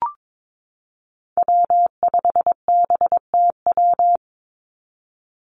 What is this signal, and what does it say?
Morse code, heard as a keyed tone, about 22 wpm, 700 Hz. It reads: W5BTW
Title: USA Call Signs 22wpm